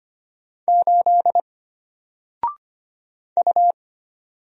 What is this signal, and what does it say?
Morse code, heard as a keyed tone, about 25 wpm, 700 Hz. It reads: 8 U
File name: Single-Letter-Number-25wpm